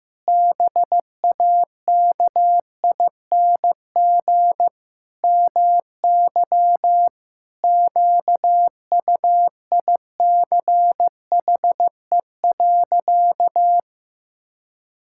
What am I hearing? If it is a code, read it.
Morse code, heard as a keyed tone, about 15 wpm, 700 Hz. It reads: BAKING MY QUICHE.
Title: Pangram Challenge v2 - 15wpm